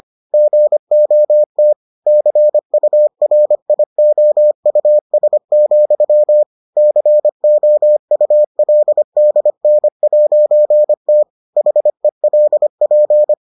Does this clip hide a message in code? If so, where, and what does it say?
Yes: Morse code from 0.3 to 13.3 s, heard as a keyed tone, about 25 wpm, 600 Hz. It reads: GOT CURIOUS, COULDN'T HELP